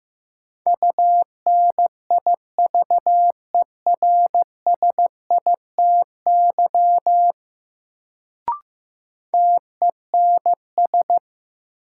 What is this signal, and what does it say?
Morse code, heard as a keyed tone, about 15 wpm, 700 Hz. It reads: UNIVERSITY TENS
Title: Sets of 4 Words - New Word Drill - 15wpm